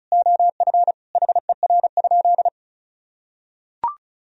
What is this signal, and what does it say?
Morse code, heard as a keyed tone, about 34 wpm, 700 Hz. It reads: OF HER?
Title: Sentences from Top 1000 Words - Review Part 4 - 35wpm